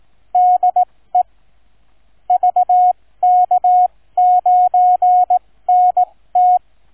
Morse code, heard as a keyed tone, about 18 wpm, spaced as for 15 wpm, 730 Hz. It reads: DE VK9NT